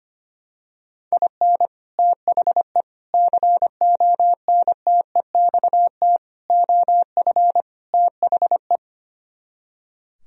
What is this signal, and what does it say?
Morse code, heard as a keyed tone, about 25 wpm, 700 Hz. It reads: IN THE CONTEXT OF THE
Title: Speed-Racing - Sets of 5 Words 25wpm